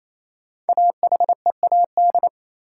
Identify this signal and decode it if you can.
Morse code, heard as a keyed tone, about 28 wpm, 700 Hz. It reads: AHEAD
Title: Sentences from Top 1000 Words - Review Part 6 - New Word Drill - 28wpm